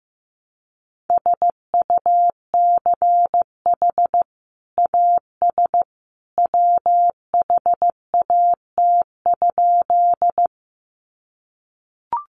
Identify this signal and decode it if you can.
Morse code, heard as a keyed tone, about 15 wpm, 700 Hz. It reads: SUCH AS WHAT?
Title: Sentences from Top 200 Words 15wpm